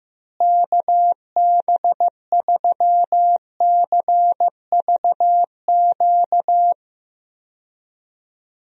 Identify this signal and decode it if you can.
Morse code, heard as a keyed tone, about 14 wpm, 700 Hz. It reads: KB3CVQ
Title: USA Call Signs - 6 Characters - Repeated 3x - 15wpm